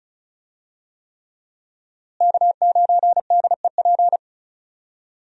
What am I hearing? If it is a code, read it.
Morse code, heard as a keyed tone, about 36 wpm, 700 Hz. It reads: K9DEP
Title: USA Call Signs - 5 Characters - Repeated 3x - 35wpm